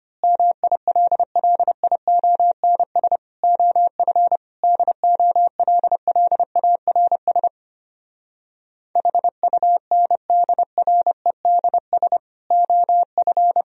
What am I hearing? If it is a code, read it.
Morse code, heard as a keyed tone, about 26 wpm, 700 Hz. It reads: MILLIONS OF DOLLARS HUNDREDS OF
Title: Speed-Racing - Sets of 5 Words 25wpm